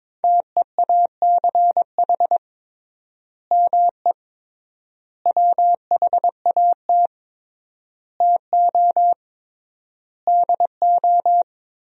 Morse code, heard as a keyed tone, about 21 wpm, 700 Hz. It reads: TEACH ME WHAT TO DO
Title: 3x Word-Spacing - Sentences from Top 500 Words - 22wpm